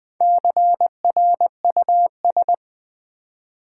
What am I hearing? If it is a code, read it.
Morse code, heard as a keyed tone, about 20 wpm, 700 Hz. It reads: CRUS
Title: Build A Word - 1 Letter At A Time - 5-Letter Words - 20wpm